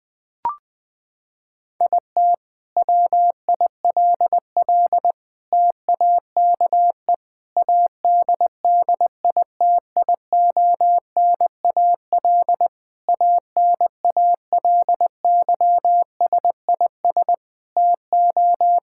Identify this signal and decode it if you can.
Morse code, heard as a keyed tone, about 20 wpm, 700 Hz. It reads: IT WILL TAKE ADDITIONAL ANALYSIS TO
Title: Sentences from Top 1000 Words - Review Part 5 - 20wpm